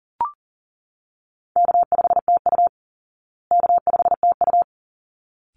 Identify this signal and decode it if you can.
Morse code, heard as a keyed tone, about 40 wpm, 700 Hz. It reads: K5TU K5TU
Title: Call Signs Repeated 3x - 40wpm